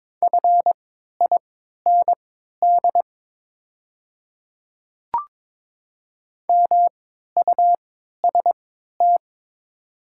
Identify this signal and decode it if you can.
Morse code, heard as a keyed tone, about 22 wpm, spaced as for 12 wpm, 700 Hz. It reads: FIND MUST
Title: 3x Character Spacing - Top 200 Words - 22wpm